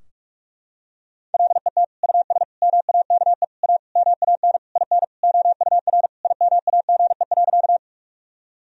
Morse code, heard as a keyed tone, about 41 wpm, 700 Hz. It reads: LET US MAKE A MAN IN OUR IMAGE.
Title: Sentences from Top 1000 Words - Review Part 4 - 45wpm